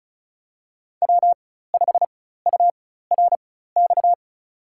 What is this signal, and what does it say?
Morse code, heard as a keyed tone, about 36 wpm, spaced as for 16 wpm, 700 Hz. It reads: W5URX
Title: USA Call Signs - 5 Characters - 4x Character Spacing - 35wpm